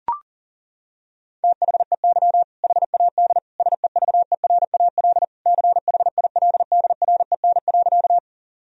Morse code, heard as a keyed tone, about 40 wpm, 700 Hz. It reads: THEY HAD SEVERAL CHILDREN.
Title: Sentences from Top 1000 Words - Review Part 2 - 40wpm